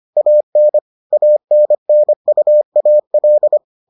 Morse code, heard as a keyed tone, about 25 wpm, 600 Hz. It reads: AN ANNUAL